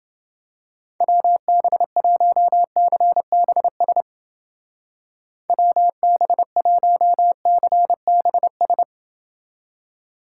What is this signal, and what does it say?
Morse code, heard as a keyed tone, about 29 wpm, 700 Hz. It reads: WB1CBS WB1CBS